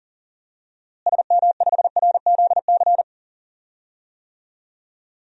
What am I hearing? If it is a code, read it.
Morse code, heard as a keyed tone, about 37 wpm, 700 Hz. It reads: SM5RZC
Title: DXCC-100 Call Signs - 6 Characters - Repeated 2x - 40wpm